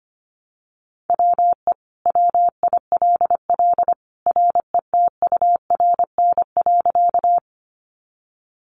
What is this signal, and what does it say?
Morse code, heard as a keyed tone, about 25 wpm, 700 Hz. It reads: WE WILL RETURN.
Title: Sentences from Top 800 Words 25wpm